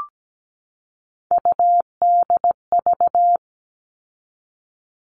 Morse code, heard as a keyed tone, about 17 wpm, 700 Hz. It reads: UDV
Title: All Cut Numbers - 3-Digit 17wpm